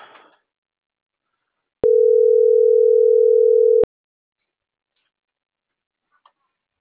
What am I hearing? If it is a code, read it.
Ringback tone (North America), 440 + 480 Hz.
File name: CENTRAL MABE / out edwinomarojeda - .13504